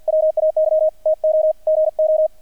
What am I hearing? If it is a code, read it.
Morse code, heard as a keyed tone, about 50 wpm, 630 Hz, with radio noise. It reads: WAY TO GO
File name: archived / CRSnet